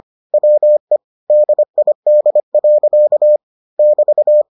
Morse code, heard as a keyed tone, about 25 wpm, 600 Hz. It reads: WE DID. <BT>